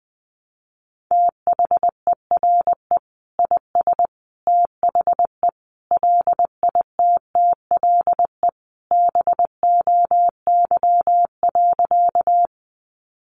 Morse code, heard as a keyed tone, about 20 wpm, 700 Hz. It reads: THERE IS THE LITTLE BOY.
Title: Sentences from Top 200 Words 20wpm